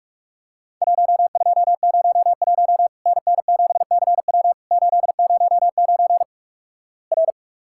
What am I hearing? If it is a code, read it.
Morse code, heard as a keyed tone, about 46 wpm, 700 Hz. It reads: 1201 NN7CW 809 R
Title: Worked All Europe DX Contest - QTC Exchange - 45wpm